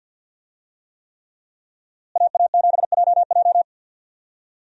Morse code, heard as a keyed tone, about 53 wpm, 700 Hz. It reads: AA7JJ
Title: USA Call Signs - 5 Characters - Repeated 3x - 50wpm